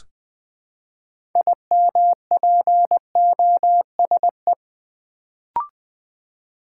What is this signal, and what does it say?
Morse code, heard as a keyed tone, about 20 wpm, 700 Hz. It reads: IMPOSE
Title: Build A Word - 2 Letters At A Time - 6-Letter Words - 20wpm